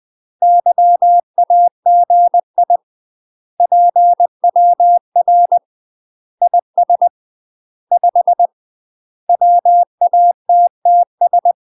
Morse code, heard as a keyed tone, about 20 wpm, 700 Hz. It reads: YAGI PWR IS 5 WATTS
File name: UK MASTER QSOs TRAINING / 20WPM / UK MASTER QSOs 0049